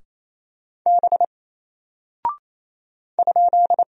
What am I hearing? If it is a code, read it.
Morse code, heard as a keyed tone, about 28 wpm, 700 Hz. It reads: B ?
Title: Single Character (Letter:Number:Punctuation:Prosign) Alt 28wpm